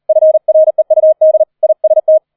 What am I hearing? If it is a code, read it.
Morse code, heard as a keyed tone, about 38 wpm, 630 Hz. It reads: FREUD IST
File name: 004 Vierter Streich 40wpm